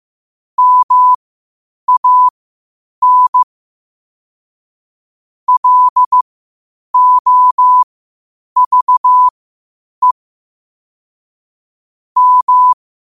Morse code, heard as a keyed tone, about 15 wpm, spaced as for 8 wpm, 1000 Hz. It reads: MAN LOVE M
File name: qotdfiles / 8wpm